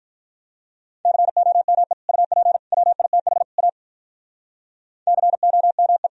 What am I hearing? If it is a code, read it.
Morse code, heard as a keyed tone, about 44 wpm, 700 Hz. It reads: COME UP WITH A COME
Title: Speed-Racing - Sets of 4 Words 35wpm